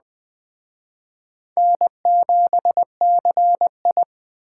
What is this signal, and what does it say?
Morse code, heard as a keyed tone, about 20 wpm, 700 Hz. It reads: N7CI
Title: Speed Racing - USA 1x2 Call Signs 17wpm